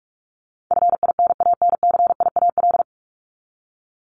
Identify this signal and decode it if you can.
Morse code, heard as a keyed tone, about 44 wpm, 700 Hz. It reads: FINANCIAL
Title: Top 751-1000 Words 45wpm